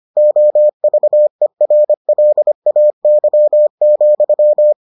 Morse code, heard as a keyed tone, about 25 wpm, 600 Hz. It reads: OVERLAY,